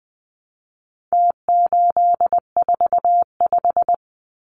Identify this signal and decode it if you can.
Morse code, heard as a keyed tone, about 20 wpm, 700 Hz. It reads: T845
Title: Numbers - Contest Style - 4-Digit 20wpm